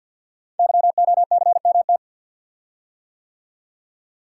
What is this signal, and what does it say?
Morse code, heard as a keyed tone, about 47 wpm, 690 Hz. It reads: YOCMT
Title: Code Groups - 5 Letters - 50wpm